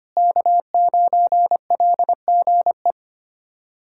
Morse code, heard as a keyed tone, about 25 wpm, 700 Hz. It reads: K9LGE
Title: USA Callsign DE Callsign 25wpm